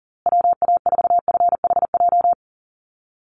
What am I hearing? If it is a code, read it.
Morse code, heard as a keyed tone, about 40 wpm, 700 Hz. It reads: WA4FHJ
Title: USA Call Signs Repeated 2x 40wpm